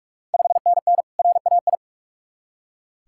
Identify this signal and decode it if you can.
Morse code, heard as a keyed tone, about 44 wpm, 690 Hz. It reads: 5NN RAI
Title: Ontario QSO Party Exchange 45wpm